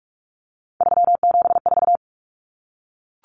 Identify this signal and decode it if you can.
Morse code, heard as a keyed tone, about 44 wpm, 700 Hz. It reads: 374
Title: Numbers - 3-Digit 45wpm